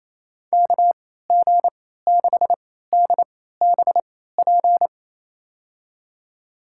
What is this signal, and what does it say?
Morse code, heard as a keyed tone, about 28 wpm, spaced as for 16 wpm, 700 Hz. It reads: KG6DBP
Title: USA Call Signs - 6 Characters - 3x Character Spacing - 28wpm